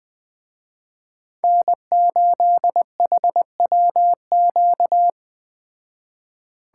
Morse code, heard as a keyed tone, about 20 wpm, 700 Hz. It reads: N8HWQ
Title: Speed Racing - USA 1x3 Call Signs 17wpm